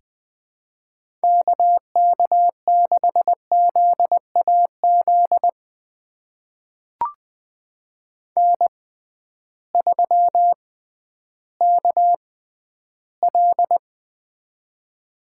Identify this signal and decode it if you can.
Morse code, heard as a keyed tone, about 20 wpm, spaced as for 12 wpm, 700 Hz. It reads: KK6ZAZ N3KL
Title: USA Call Signs - 6x Character Spacing - 20wpm